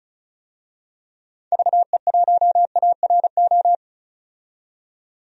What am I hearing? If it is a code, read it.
Morse code, heard as a keyed tone, about 36 wpm, 700 Hz. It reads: VE1ARO
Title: DXCC-100 Call Signs - 6 Characters - 35wpm